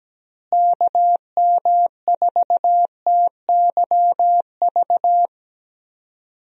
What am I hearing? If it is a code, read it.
Morse code, heard as a keyed tone, about 17 wpm, 700 Hz. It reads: KM4TYV
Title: USA Call Signs - 6 Characters - Repeated 3x - 17wpm